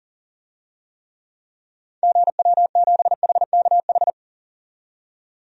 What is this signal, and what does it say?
Morse code, heard as a keyed tone, about 37 wpm, 700 Hz. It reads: GW7HKH
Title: DXCC-100 Call Signs - 6 Characters - 40wpm